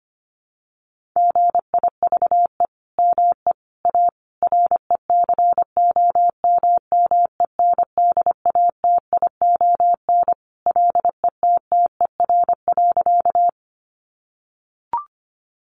Morse code, heard as a keyed tone, about 25 wpm, 700 Hz. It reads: GIVE ME A RECOMMENDATION LETTER.
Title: Sentences from Top 1400 Words 25wpm